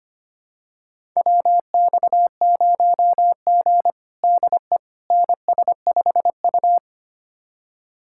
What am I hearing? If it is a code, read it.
Morse code, heard as a keyed tone, about 25 wpm, 700 Hz. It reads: WX0G DE NS5U
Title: USA Callsign DE Callsign - 4 Characters - 25wpm